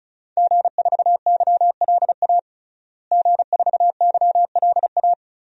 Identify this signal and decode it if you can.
Morse code, heard as a keyed tone, about 36 wpm, 700 Hz. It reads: G4YLA G4YLA
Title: DXCC-100 Call Signs - 5 Characters - Repeated 3x - 35wpm